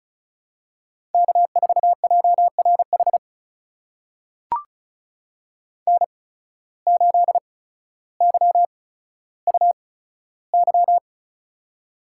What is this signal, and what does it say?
Morse code, heard as a keyed tone, about 34 wpm, spaced as for 15 wpm, 700 Hz. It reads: K4JRH N8YUY
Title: USA Call Signs - 5 Characters - 8x Character Spacing - 35wpm